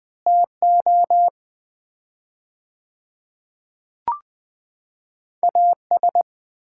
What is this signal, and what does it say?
Morse code, heard as a keyed tone, about 20 wpm, 700 Hz. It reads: TO AS
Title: 4x Word-Spacing - Sets of 4 Words - 20wpm